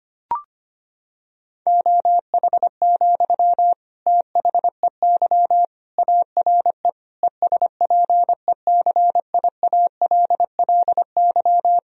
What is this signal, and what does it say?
Morse code, heard as a keyed tone, about 25 wpm, 700 Hz. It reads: OH, THEY ARE ESPECIALLY
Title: Sentences from Top 1000 Words - Review Part 4 - 25wpm